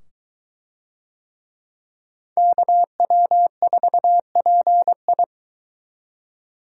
Morse code, heard as a keyed tone, about 23 wpm, 700 Hz. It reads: KW4PI